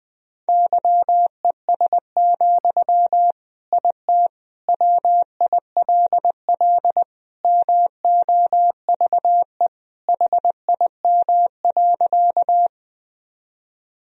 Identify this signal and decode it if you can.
Morse code, heard as a keyed tone, about 20 wpm, 700 Hz. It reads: YES, IT WILL MOVE HIM.
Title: Sentences from Top 1000 Words - Review Part 1 - 20wpm